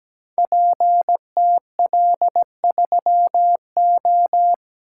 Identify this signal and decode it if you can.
Morse code, heard as a keyed tone, about 17 wpm, 700 Hz. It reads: PTL3O
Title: Code Groups - 5 Letter:Numbers - 17wpm